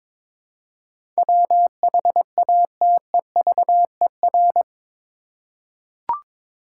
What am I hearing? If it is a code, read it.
Morse code, heard as a keyed tone, about 22 wpm, 700 Hz. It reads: WHATEVER
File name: practice / 22wpm / Eight Letter Words - 22wpm